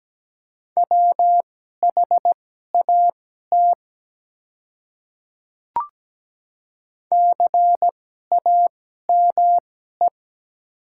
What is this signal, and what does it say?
Morse code, heard as a keyed tone, about 17 wpm, spaced as for 12 wpm, 700 Hz. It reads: WHAT CAME
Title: 2x Character Spacing - Top 200 Words - 17wpm